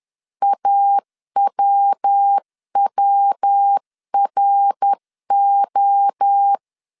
Morse code, heard as a keyed tone, about 10 wpm, 800 Hz. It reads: AWWRO
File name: 10wpm